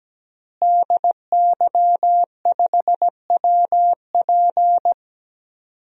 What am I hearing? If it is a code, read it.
Morse code, heard as a keyed tone, about 17 wpm, 700 Hz. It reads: DY5WP
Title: DXCC-100 Call Signs - 5 Characters - Repeated 3x - 17wpm